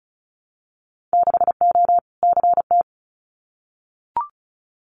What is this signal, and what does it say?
Morse code, heard as a keyed tone, about 34 wpm, 700 Hz. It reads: 6O CT
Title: Winter Field Day Exchanges Encore 35wpm